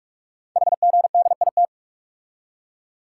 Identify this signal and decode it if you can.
Morse code, heard as a keyed tone, about 44 wpm, 700 Hz. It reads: HGDIT
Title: Code Groups - 5 Letters - 45wpm